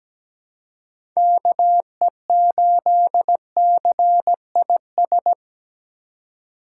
Morse code, heard as a keyed tone, about 17 wpm, 700 Hz. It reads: KE8CIS
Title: USA Call Signs - 6 Characters - 2x Character Spacing - 17wpm